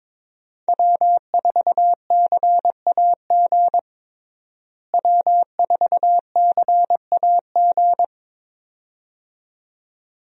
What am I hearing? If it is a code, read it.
Morse code, heard as a keyed tone, about 22 wpm, 700 Hz. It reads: W4CAG W4CAG
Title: USA Call Signs - 5 Characters - Repeated 2x - 22wpm